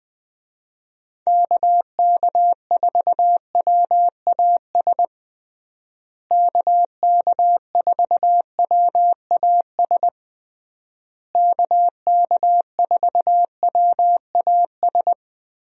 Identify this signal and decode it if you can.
Morse code, heard as a keyed tone, about 20 wpm, 700 Hz. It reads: KK4WAS KK4WAS KK4WAS
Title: USA Call Signs - 6 Characters - Repeated 3x - 20wpm